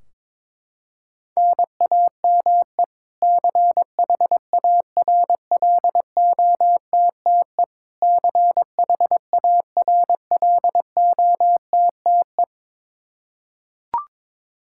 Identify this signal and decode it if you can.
Morse code, heard as a keyed tone, about 22 wpm, 700 Hz. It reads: NAME CHARLOTTE CHARLOTTE
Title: Practice QSO - Exchange 1 - Beginning - 22wpm